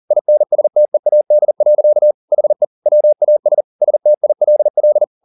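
Morse code, heard as a keyed tone, about 40 wpm, 600 Hz. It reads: INSTEAD. HE WAS STILL
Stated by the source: Morse code at 40 wpm, practice text, CW med Gnister.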